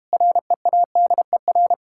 Morse code, heard as a keyed tone, about 33 wpm, 700 Hz. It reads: READER